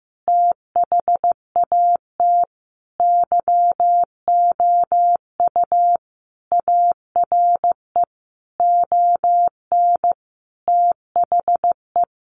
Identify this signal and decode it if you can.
Morse code, heard as a keyed tone, about 15 wpm, 700 Hz. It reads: THAT YOU ARE ON THE
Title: Sentences from Top 1400 Words 15wpm